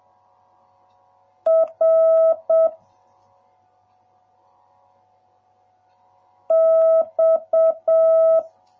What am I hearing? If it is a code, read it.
Morse code, heard as a keyed tone, about 7 wpm, 650 Hz. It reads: R X